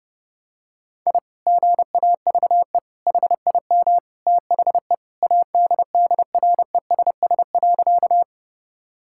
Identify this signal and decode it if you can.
Morse code, heard as a keyed tone, about 30 wpm, 700 Hz. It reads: I GAVE HIM THE ADDRESS.
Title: Sentences from Top 1000 Words - Review Part 6 - 30wpm